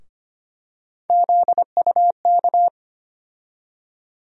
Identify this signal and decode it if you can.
Morse code, heard as a keyed tone, about 25 wpm, 700 Hz. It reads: ZUK
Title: USA Call Sign Postfixes - 3 Letters 25wpm